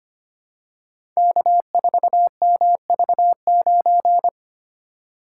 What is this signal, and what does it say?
Morse code, heard as a keyed tone, about 25 wpm, 690 Hz. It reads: K4MV9
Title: Code Groups - 5 Letter:Numbers - 25wpm